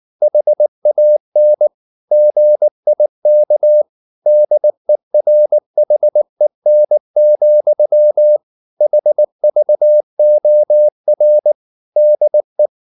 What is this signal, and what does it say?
Morse code, heard as a keyed tone, about 19 wpm, 600 Hz. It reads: HAN GIK DERHEN, HVOR DE
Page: Ferie 19 wpm